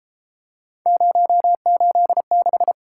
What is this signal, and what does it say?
Morse code, heard as a keyed tone, about 33 wpm, 700 Hz. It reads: 086